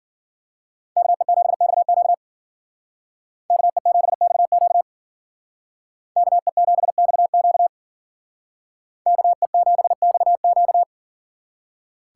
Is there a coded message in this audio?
Yes: Morse code from 1.0 to 10.8 s, heard as a keyed tone, about 52 wpm, 700 Hz. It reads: KE7XQ KE7XQ KE7XQ KE7XQ